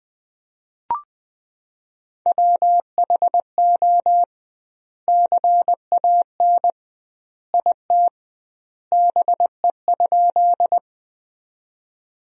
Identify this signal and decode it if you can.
Morse code, heard as a keyed tone, about 21 wpm, 700 Hz. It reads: WHO CAN IT BE?
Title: 2x Word-Spacing - Sentences from Top 100 Words - 20wpm